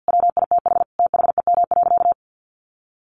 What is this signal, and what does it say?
Morse code, heard as a keyed tone, about 50 wpm, 700 Hz. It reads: WITH THEM.
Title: Sentences from Top 1500 Words 50wpm